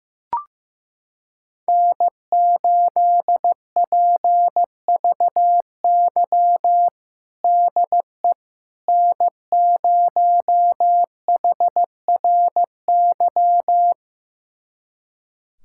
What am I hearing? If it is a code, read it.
Morse code, heard as a keyed tone, about 15 wpm, 700 Hz. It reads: N8PVY DE N0HRY
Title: USA Callsign DE Callsign - 5 Characters - 15wpm